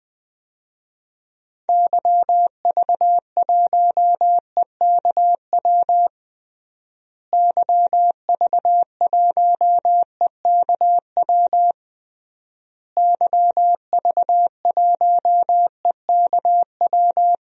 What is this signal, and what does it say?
Morse code, heard as a keyed tone, about 20 wpm, 700 Hz. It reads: YV1EKW YV1EKW YV1EKW
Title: DXCC-100 Call Signs - 6 Characters - Repeated 3x - 20wpm